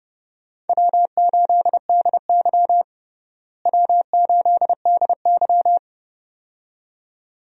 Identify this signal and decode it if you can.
Morse code, heard as a keyed tone, about 30 wpm, 700 Hz. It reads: W8DY W8DY
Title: USA Call Signs - 4 Characters - Repeated 2x - 30wpm